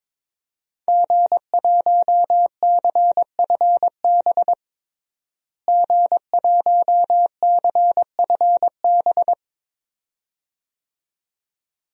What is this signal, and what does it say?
Morse code, heard as a keyed tone, about 22 wpm, 700 Hz. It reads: G1CFB G1CFB
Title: DXCC-100 Call Signs - 5 Characters - Repeated 2x - 22wpm